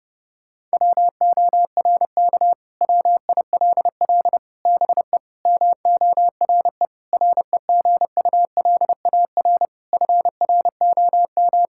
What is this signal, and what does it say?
Morse code, heard as a keyed tone, about 32 wpm, 700 Hz. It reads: WORK WILL BE MORE REGULAR FROM
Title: Sentences from Top 1000 Words - Review Part 6 - 30wpm